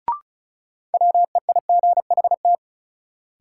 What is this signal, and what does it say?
Morse code, heard as a keyed tone, about 36 wpm, 700 Hz. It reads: WEIGHT
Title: Sentences from Top 800 Words - New Word Drill - 35wpm